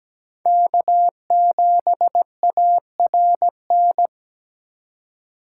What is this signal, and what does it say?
Morse code, heard as a keyed tone, about 20 wpm, 700 Hz. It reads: K7ARN
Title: Speed Racing - USA 1x3 Call Signs 17wpm